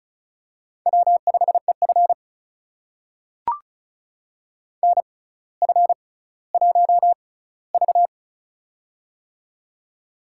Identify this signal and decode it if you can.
Morse code, heard as a keyed tone, about 36 wpm, spaced as for 18 wpm, 700 Hz. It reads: W5EF NF1V
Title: USA Call Signs - 4 Characters - 6x Character Spacing - 35wpm